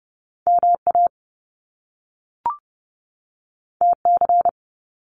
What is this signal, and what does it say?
Morse code, heard as a keyed tone, about 30 wpm, 700 Hz. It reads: MA TC
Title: Top Two Letter Combinations in Words 30wpm